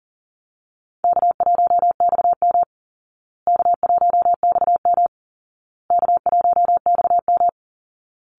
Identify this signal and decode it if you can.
Morse code, heard as a keyed tone, about 40 wpm, 700 Hz. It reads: K1XM K1XM K1XM
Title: Call Signs Repeated 2x - 40wpm